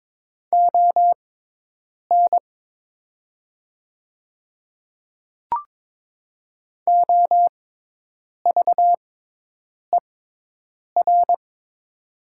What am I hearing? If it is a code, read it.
Morse code, heard as a keyed tone, about 22 wpm, spaced as for 8 wpm, 700 Hz. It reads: ON OVER
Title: 6x Character Spacing - Top 100 Words - 22wpm